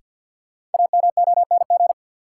Morse code, heard as a keyed tone, about 53 wpm, 700 Hz. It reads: AMONG
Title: Sets of 2 Words - New Word Drill - 50wpm